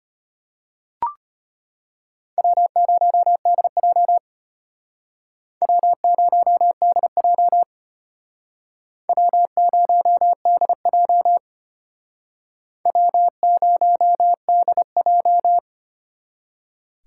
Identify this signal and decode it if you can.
Morse code, heard as a keyed tone, about 32 wpm, 700 Hz. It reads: W0DJ W0DJ W0DJ W0DJ